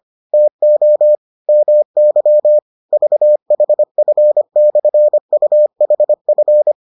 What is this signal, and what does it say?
Morse code, heard as a keyed tone, about 25 wpm, 600 Hz. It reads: TO MY VHF/UHF